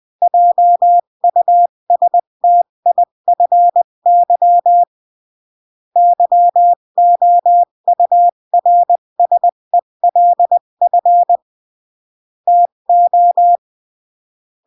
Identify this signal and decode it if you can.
Morse code, heard as a keyed tone, about 20 wpm, 700 Hz. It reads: JUSTIFY YOURSELF TO
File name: CW / jodorowsky / 82 commandments 20-15